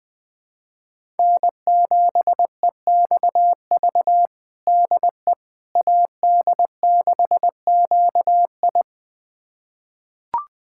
Morse code, heard as a keyed tone, about 21 wpm, 700 Hz. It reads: N7EXV DE AD6QI
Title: USA Callsign DE Callsign - 5 Characters - 20wpm